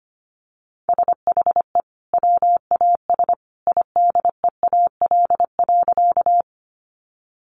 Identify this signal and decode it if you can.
Morse code, heard as a keyed tone, about 25 wpm, 700 Hz. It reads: SHE WAS IDEAL.
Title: Sentences from Top 1400 Words 25wpm